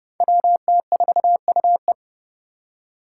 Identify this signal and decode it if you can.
Morse code, heard as a keyed tone, about 30 wpm, 700 Hz. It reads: WT4UE
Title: Five Challenging Letter-Number Combinations 30wpm